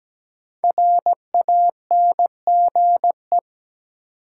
Morse code, heard as a keyed tone, about 17 wpm, 700 Hz. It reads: RANGE
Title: Sets of 4 Words - New Word Drill - 17wpm